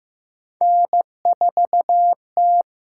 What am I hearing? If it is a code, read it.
Morse code, heard as a keyed tone, about 15 wpm, 700 Hz. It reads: N 4 T
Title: USA Call Signs - 6x Character Spacing - 15wpm